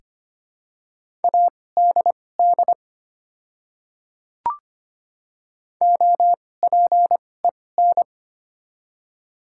Morse code, heard as a keyed tone, about 25 wpm, spaced as for 18 wpm, 700 Hz. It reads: ADD OPEN
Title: 2x Character Spacing - Top 300 Words - 25wpm